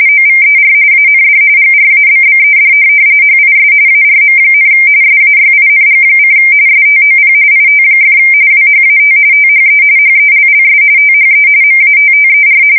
RTTY, about 45.45 baud. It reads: OVER AND GRABBED HIS ARM, SAYING: STOP IT! WHAT THE HELLS THE MATTER
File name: fsk170 / USS-New-Jersey